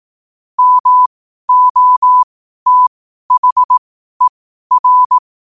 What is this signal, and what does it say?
Morse code, heard as a keyed tone, about 18 wpm, spaced as for 12 wpm, 1000 Hz. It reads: MOTHER
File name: qotdfiles / 12wpm